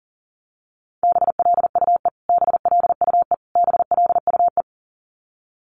Morse code, heard as a keyed tone, about 40 wpm, 700 Hz. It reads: BLUE BLUE BLUE
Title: Top 500 Words Repeated 40wpm